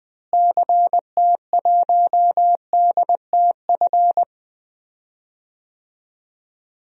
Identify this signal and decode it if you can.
Morse code, heard as a keyed tone, about 21 wpm, 700 Hz. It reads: CT1DTF
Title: DXCC-100 Call Signs - 6 Characters - Repeated 2x - 20wpm